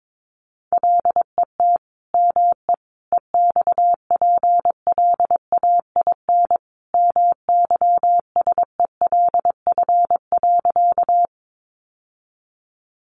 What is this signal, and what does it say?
Morse code, heard as a keyed tone, about 22 wpm, 700 Hz. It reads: LET ME EXPLAIN MYSELF.
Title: Sentences from Top 1300 Words 22wpm